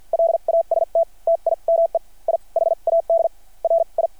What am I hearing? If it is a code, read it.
Morse code, heard as a keyed tone, about 50 wpm, spaced as for 40 wpm, 670 Hz. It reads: LAST TIME I HAD RI
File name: archived / CRSnet